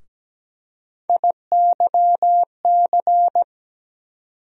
Morse code, heard as a keyed tone, about 17 wpm, 700 Hz. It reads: IYC
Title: USA Call Sign Postfixes - 3 Letters 17wpm